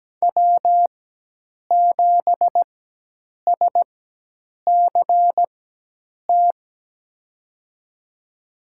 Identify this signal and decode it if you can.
Morse code, heard as a keyed tone, about 17 wpm, spaced as for 8 wpm, 700 Hz. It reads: W7SCT
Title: USA Call Signs - 5 Characters - 4x Character Spacing - 17wpm